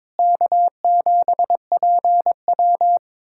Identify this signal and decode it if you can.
Morse code, heard as a keyed tone, about 22 wpm, 700 Hz. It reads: K7PW
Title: USA Callsign DE Callsign - 4 Characters - 22wpm